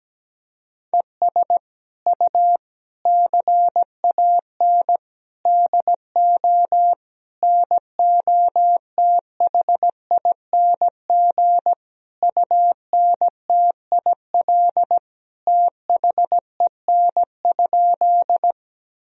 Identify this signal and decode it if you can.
Morse code, heard as a keyed tone, about 17 wpm, 700 Hz. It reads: ES U CAN DO NOTHING UNTIL THEN?
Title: Sentences from Top 500 Words and Abbreviations 17wpm